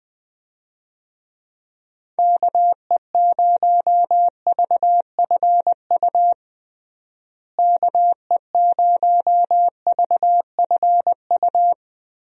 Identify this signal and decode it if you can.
Morse code, heard as a keyed tone, about 20 wpm, 700 Hz. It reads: KE0VFU KE0VFU
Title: USA Call Signs - 6 Characters - Repeated 2x - 20wpm